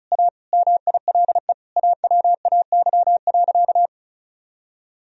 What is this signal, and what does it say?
Morse code, heard as a keyed tone, about 34 wpm, 700 Hz. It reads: A MILE AWAY.
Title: Sentences from Top 1000 Words - Review Part 1 - 35wpm